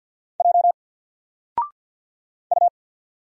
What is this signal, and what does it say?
Morse code, heard as a keyed tone, about 50 wpm, 700 Hz. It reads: J U
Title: Single Letter-Number 50wpm